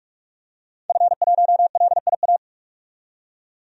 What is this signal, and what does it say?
Morse code, heard as a keyed tone, about 46 wpm, 700 Hz. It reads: F1LIA
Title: DXCC-100 Call Signs - 5 Characters - 45wpm